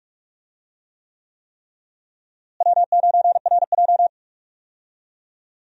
Morse code, heard as a keyed tone, about 47 wpm, 700 Hz. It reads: W9RJ
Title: USA Call Signs - 4 Characters - Repeated 3x - 45wpm